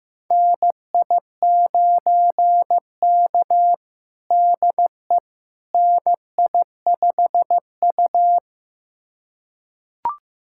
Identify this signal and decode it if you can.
Morse code, heard as a keyed tone, about 15 wpm, 700 Hz. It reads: NI9K DE NI5U
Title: USA Callsign DE Callsign - 4 Characters - 15wpm